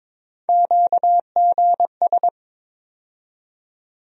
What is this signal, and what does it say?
Morse code, heard as a keyed tone, about 22 wpm, 700 Hz. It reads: QGS
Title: USA Call Sign Postfixes - 3 Letters 22wpm